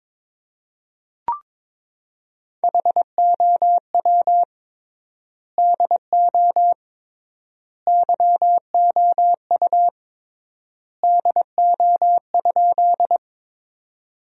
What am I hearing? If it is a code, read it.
Morse code, heard as a keyed tone, about 22 wpm, 700 Hz. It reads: HOW DO YOU DO?
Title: 3x Word-Spacing - Sentences from Top 100 Words - 22wpm